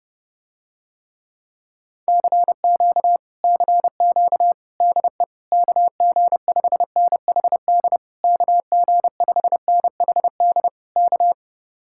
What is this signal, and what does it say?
Morse code, heard as a keyed tone, about 32 wpm, 700 Hz. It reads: CQ CQ DE KG5NHD KG5NHD K